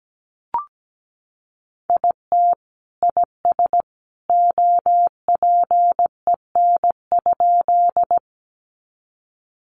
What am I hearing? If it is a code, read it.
Morse code, heard as a keyed tone, about 17 wpm, 700 Hz. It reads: IT IS OPEN?
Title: Sentences from Top 300 Words 17wpm